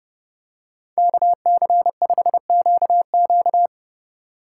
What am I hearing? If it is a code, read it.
Morse code, heard as a keyed tone, about 30 wpm, 700 Hz. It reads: KC5QQ